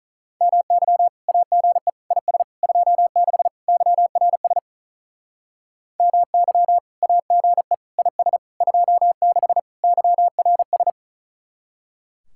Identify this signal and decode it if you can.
Morse code, heard as a keyed tone, about 46 wpm, 700 Hz. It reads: MY AGE IS 26 YRS MY AGE IS 26 YRS